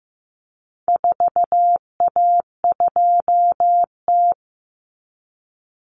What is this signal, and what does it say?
Morse code, heard as a keyed tone, about 15 wpm, 700 Hz. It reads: 4A2T
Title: Common Cut Numbers - 4-Digit 15wpm